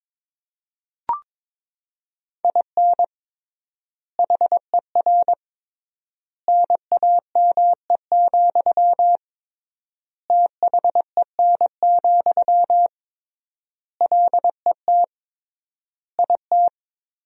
Morse code, heard as a keyed tone, about 22 wpm, 700 Hz. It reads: IN HER NAME, THEN, LET IT
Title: 3x Word-Spacing - Sentences from Top 500 Words - 22wpm